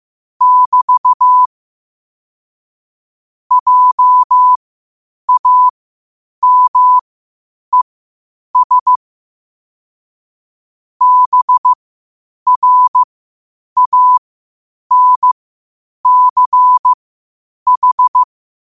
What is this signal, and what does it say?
Morse code, heard as a keyed tone, about 15 wpm, spaced as for 8 wpm, 1000 Hz. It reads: <BT> JAMES BRANCH
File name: qotdfiles / 8wpm